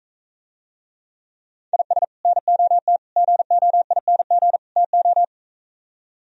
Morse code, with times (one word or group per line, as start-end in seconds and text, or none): 1.7-2.0 IS
2.2-3.0 NOT
3.2-4.6 GOING
4.8-5.2 TO